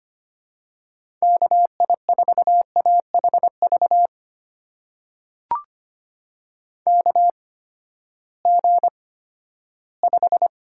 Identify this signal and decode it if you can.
Morse code, heard as a keyed tone, about 25 wpm, spaced as for 14 wpm, 700 Hz. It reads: KI4AHV KG5
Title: USA Call Signs - 6 Characters - 8x Character Spacing - 25wpm